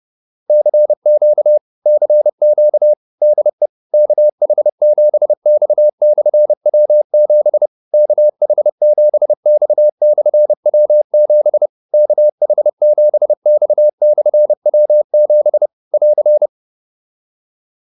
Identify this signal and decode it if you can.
Morse code, heard as a keyed tone, about 30 wpm, 600 Hz. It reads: CQ CQ DE KH7X/W7 KH7X/W7 KH7X/W7 <AR>